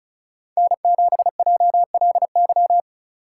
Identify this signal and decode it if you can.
Morse code, heard as a keyed tone, about 34 wpm, 700 Hz. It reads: N7JLY
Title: USA Call Signs - 2x Character Spacing - 35wpm